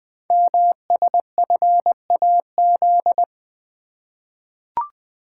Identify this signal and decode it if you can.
Morse code, heard as a keyed tone, about 20 wpm, 700 Hz. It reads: MSFAZ
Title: Code Groups - 5 Letters - 20wpm